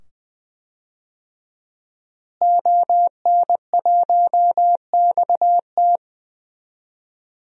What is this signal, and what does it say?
Morse code, heard as a keyed tone, about 20 wpm, 700 Hz. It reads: ON1XT
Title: DXCC-100 Call Signs - 5 Characters - Repeated 2x - 20wpm